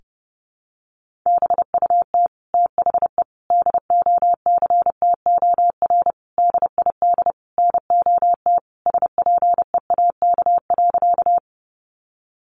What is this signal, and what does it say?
Morse code, heard as a keyed tone, about 30 wpm, 700 Hz. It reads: BUT THE DOCTOR DID NOT SPEAK.
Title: Sentences from Top 1500 Words 30wpm